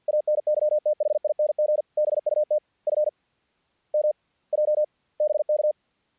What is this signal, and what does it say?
Morse code, heard as a keyed tone, about 50 wpm, spaced as for 33 wpm, 610 Hz. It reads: ANYTHING BUT FMJ BK
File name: archived / CRSnet